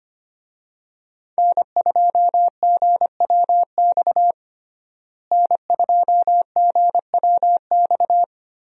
Morse code, heard as a keyed tone, about 25 wpm, 700 Hz. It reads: N2GWX N2GWX
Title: USA Call Signs - 5 Characters - Repeated 2x - 25wpm